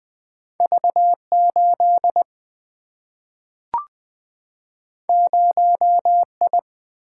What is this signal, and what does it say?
Morse code, heard as a keyed tone, about 20 wpm, 700 Hz. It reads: V8 0I
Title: Single-Letter Single-Number 20wpm